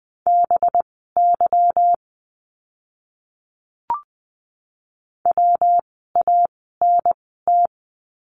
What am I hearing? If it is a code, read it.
Morse code, heard as a keyed tone, about 20 wpm, spaced as for 14 wpm, 700 Hz. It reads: BY WANT
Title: 206 Lesson 33f - Letter j - Words 20wpm